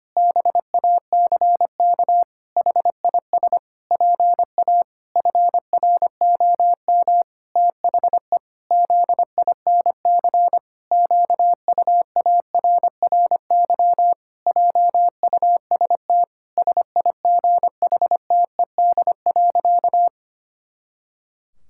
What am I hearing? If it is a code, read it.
Morse code, heard as a keyed tone, about 25 wpm, 700 Hz. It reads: BACK HIS PA FROM THE ZINC QUARRY JUST SIGHTED.
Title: Pangram Challenge v2 - 25wpm